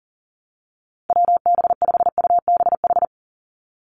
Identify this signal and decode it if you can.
Morse code, heard as a keyed tone, about 40 wpm, 690 Hz. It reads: WB5UBH
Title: USA Call Signs 40wpm